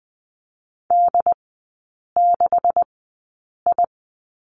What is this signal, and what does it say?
Morse code, heard as a keyed tone, about 20 wpm, spaced as for 8 wpm, 700 Hz. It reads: D6I
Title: 178 Lesson 29e - Slash - Three Characters 20wpm